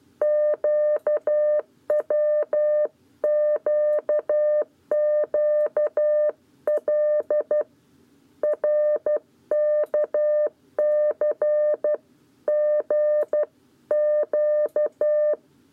Morse code, heard as a keyed tone, about 12 wpm, 600 Hz. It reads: QWQQL RKC G Q
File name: 7b lezione cw 1